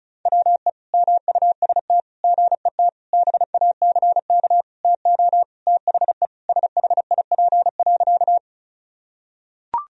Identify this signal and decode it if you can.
Morse code, heard as a keyed tone, about 34 wpm, 700 Hz. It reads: WE MUST GET BACK TO THE SHIP.
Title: Sentences from Top 1000 Words - Review Part 2 - 35wpm